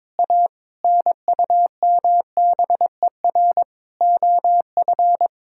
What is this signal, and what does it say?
Morse code, heard as a keyed tone, about 22 wpm, 700 Hz. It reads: A NUMBER OF
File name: practice / 22wpm / Sets of 6 Words - 22wpm